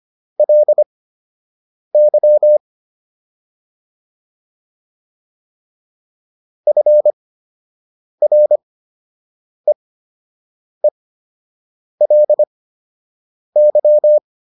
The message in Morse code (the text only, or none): LY FREELY